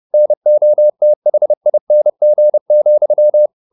Morse code, heard as a keyed tone, about 30 wpm, 600 Hz. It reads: NOTHING,